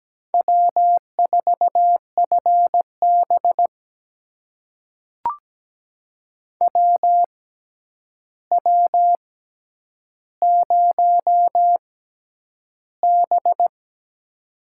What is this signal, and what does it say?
Morse code, heard as a keyed tone, about 17 wpm, spaced as for 9 wpm, 700 Hz. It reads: W4FB WW0B
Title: USA Call Signs - 4 Characters - 6x Character Spacing - 17wpm